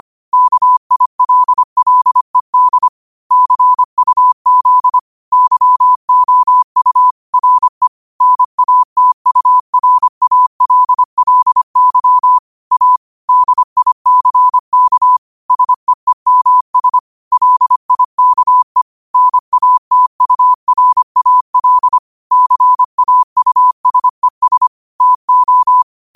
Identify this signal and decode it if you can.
Morse code, heard as a keyed tone, about 25 wpm, 1000 Hz. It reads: KILLED CUZ YOU RE NATURALLY A DICK SEEMS LIKE NATURAL CAUSES TO
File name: 25wpm